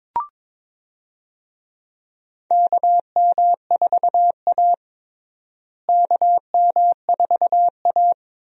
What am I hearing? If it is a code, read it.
Morse code, heard as a keyed tone, about 21 wpm, 700 Hz. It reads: KM4A KM4A
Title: USA Call Signs - 4 Characters - Repeated 2x - 22wpm